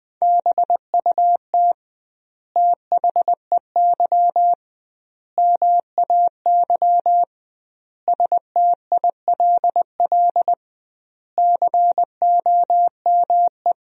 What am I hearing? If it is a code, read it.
Morse code, heard as a keyed tone, about 21 wpm, 700 Hz. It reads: BUT THEY MAY STILL COME
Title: 2x Word-Spacing - Sentences from Top 300 Words - 20wpm